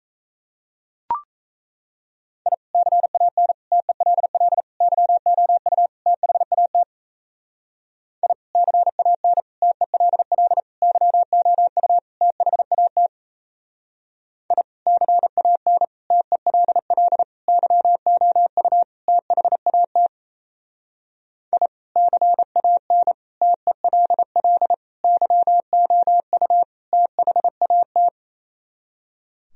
Morse code, heard as a keyed tone, about 35 wpm, 700 Hz. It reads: I CAN TELL YOU THAT I CAN TELL YOU THAT I CAN TELL YOU THAT I CAN TELL YOU THAT
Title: Speed-Racing - Sets of 5 Words 28wpm